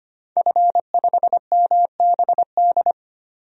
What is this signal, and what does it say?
Morse code, heard as a keyed tone, about 25 wpm, 700 Hz. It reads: F5MBD
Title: DXCC-100 Call Signs - 5 Characters - 25wpm